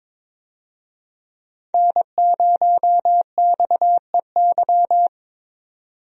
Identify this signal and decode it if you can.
Morse code, heard as a keyed tone, about 22 wpm, 700 Hz. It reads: N0XEY
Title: USA Call Signs - 5 Characters - 22wpm